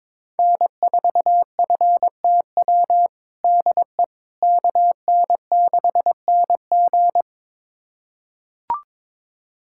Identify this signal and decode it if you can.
Morse code, heard as a keyed tone, about 22 wpm, 700 Hz. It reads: N4FTW DE KN6NG
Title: USA Callsign DE Callsign - 5 Characters - 22wpm